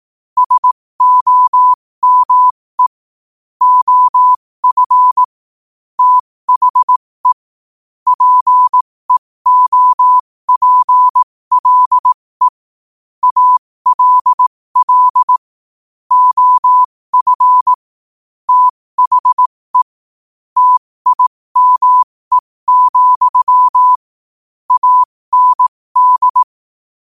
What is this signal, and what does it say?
Morse code, heard as a keyed tone, about 18 wpm, spaced as for 15 wpm, 1000 Hz. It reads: SOME OF THE PEOPLE ALL OF THE TIME, AND
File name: 15wpm